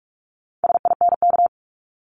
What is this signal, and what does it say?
Morse code, heard as a keyed tone, about 46 wpm, 700 Hz. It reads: SINK
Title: Multi-letter Phonemes 45wpm